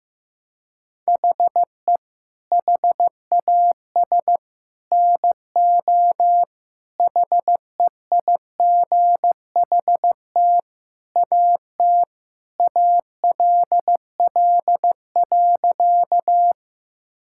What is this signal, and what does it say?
Morse code, heard as a keyed tone, about 16 wpm, 700 Hz. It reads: HE HAS NO HEIGHT AT ALL.
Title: Sentences from Top 1600 Words 15wpm